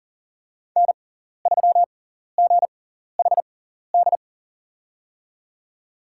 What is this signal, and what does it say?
Morse code, heard as a keyed tone, about 40 wpm, spaced as for 14 wpm, 700 Hz. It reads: N3GHD
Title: USA Call Signs - 6x Character Spacing - 40wpm